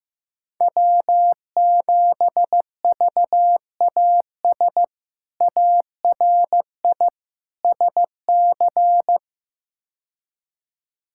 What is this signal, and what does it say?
Morse code, heard as a keyed tone, about 15 wpm, 700 Hz. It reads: W7VAS ARI SC
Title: K1USN SST QSO Practice - 15wpm